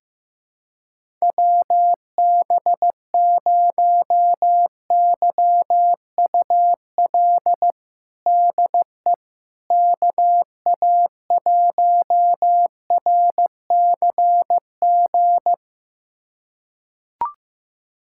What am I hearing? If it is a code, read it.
Morse code, heard as a keyed tone, about 15 wpm, 700 Hz. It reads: WB0YUL DE KA1RCG
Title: USA Callsign DE Callsign 15wpm